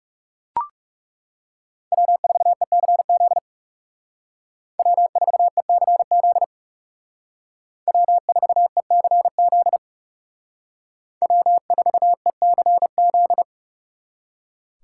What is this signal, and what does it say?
Morse code, heard as a keyed tone, about 39 wpm, 700 Hz. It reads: W4ECZ W4ECZ W4ECZ W4ECZ